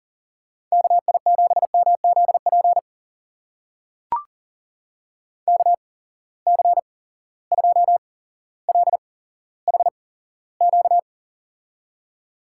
Morse code, heard as a keyed tone, about 39 wpm, spaced as for 17 wpm, 700 Hz. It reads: KI7MZP KC2LHQ
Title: USA Call Signs - 6 Characters - 8x Character Spacing - 40wpm